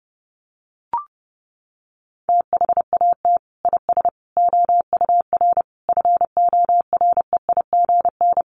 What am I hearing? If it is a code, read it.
Morse code, heard as a keyed tone, about 30 wpm, 700 Hz. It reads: THAT IS OUR FOREIGN
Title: Sentences from Top 1400 Words 30wpm